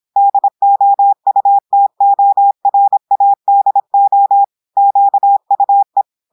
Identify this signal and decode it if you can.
Morse code, heard as a keyed tone, about 26 wpm, 800 Hz. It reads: DOUTORADO QUE
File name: audios / portugais / portugais texte en clair 1 vitesse 26 mots minutes